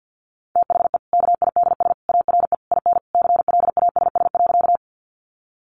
Morse code, heard as a keyed tone, about 50 wpm, 700 Hz. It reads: THE KIDS ARE IN CLASS.
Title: Easy Phrases 50wpm